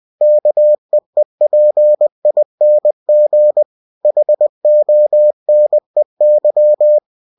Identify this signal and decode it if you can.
Morse code, heard as a keyed tone, about 20 wpm, 600 Hz. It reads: KEEPING HONEY
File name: cw-books / winnie-the-pooh / winnie-20wpm-page-0051